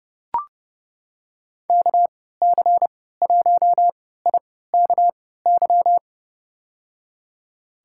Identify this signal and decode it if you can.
Morse code, heard as a keyed tone, about 31 wpm, spaced as for 17 wpm, 700 Hz. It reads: KC1IKY
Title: USA Call Signs - 6 Characters - 3x Character Spacing - 30wpm